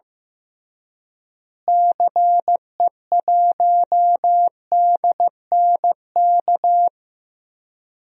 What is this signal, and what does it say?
Morse code, heard as a keyed tone, about 15 wpm, 700 Hz. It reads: CE1DNK
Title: DXCC-100 Call Signs - 6 Characters - Repeated 3x - 15wpm